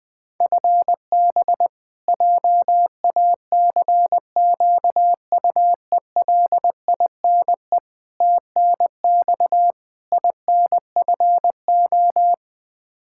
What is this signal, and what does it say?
Morse code, heard as a keyed tone, about 20 wpm, 700 Hz. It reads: FB JACQUELINE TNX INFO
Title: Practice QSO - Exchange 3 - More Info - 20wpm